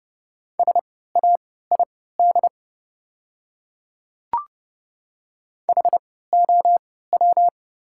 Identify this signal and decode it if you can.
Morse code, heard as a keyed tone, about 30 wpm, spaced as for 17 wpm, 700 Hz. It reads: SAID HOW
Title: 3x Character Spacing - Top 100 Words - 30wpm